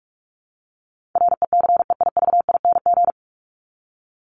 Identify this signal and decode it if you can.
Morse code, heard as a keyed tone, about 44 wpm, 700 Hz. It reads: RECEIVING
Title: ING Suffix - Top 100 Words with Base Words 45wpm